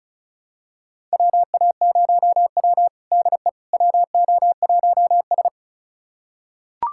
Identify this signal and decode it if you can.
Morse code, heard as a keyed tone, about 36 wpm, 700 Hz. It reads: WA0W DE WO1S
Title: USA Callsign DE Callsign - 4 Characters - 35wpm